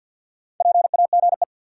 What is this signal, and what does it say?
Morse code, heard as a keyed tone, about 53 wpm, 700 Hz. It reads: PAGE
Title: Four Letter Words - 50wpm